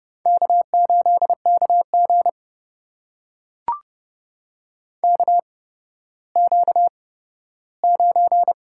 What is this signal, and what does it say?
Morse code, heard as a keyed tone, about 32 wpm, spaced as for 15 wpm, 700 Hz. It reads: K8KG KQ9
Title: USA Call Signs - 4 Characters - 8x Character Spacing - 30wpm